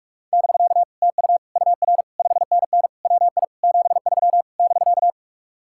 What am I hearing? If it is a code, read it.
Morse code, heard as a keyed tone, about 40 wpm, 700 Hz. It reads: <BK> TU UR 5NN WI 73 <BK>